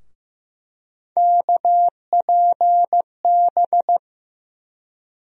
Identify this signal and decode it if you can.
Morse code, heard as a keyed tone, about 15 wpm, 700 Hz. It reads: KPB
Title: USA Call Sign Postfixes - 3 Letters 15wpm